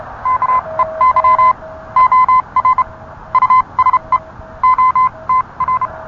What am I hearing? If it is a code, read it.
Morse code, heard as a keyed tone, about 30 wpm, spaced as for 25 wpm, 1030 Hz. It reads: KEY OR USE OTR